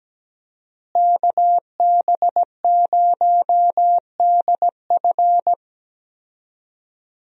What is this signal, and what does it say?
Morse code, heard as a keyed tone, about 17 wpm, 700 Hz. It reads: KB0DF
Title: USA Call Signs - 5 Characters - Repeated 2x - 17wpm